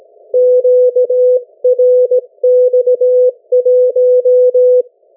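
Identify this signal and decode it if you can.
Morse code, heard as a keyed tone, about 17 wpm, 500 Hz. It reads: QRX1
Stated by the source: Morse code from the national amateur radio CW net of New Zealand, which meets on 3535.0 kHz or 7030 kHz.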